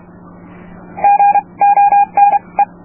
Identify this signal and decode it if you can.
Morse code, heard as a keyed tone, about 31 wpm, spaced as for 25 wpm, 810 Hz. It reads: GONE